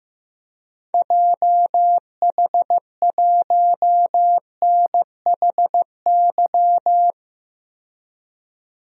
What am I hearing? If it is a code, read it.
Morse code, heard as a keyed tone, about 15 wpm, 700 Hz. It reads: JH1NHY
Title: DXCC-100 Call Signs - 6 Characters - 15wpm